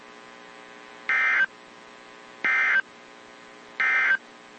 Emergency Alert System (SAME) tones, heard as harsh buzzing bursts. It is heard from 1.0 to 4.3 s.
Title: WZ2523 - NOAA Weather Radio